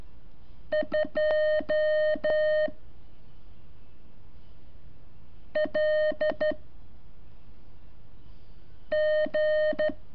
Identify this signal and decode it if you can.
Morse code, heard as a keyed tone, about 12 wpm, 610 Hz. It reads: 2 L G
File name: curso / ejercicio 24